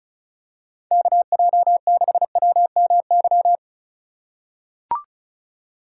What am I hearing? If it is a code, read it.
Morse code, heard as a keyed tone, about 34 wpm, 700 Hz. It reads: KJ6WMY
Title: USA Call Signs - 2x Character Spacing - 35wpm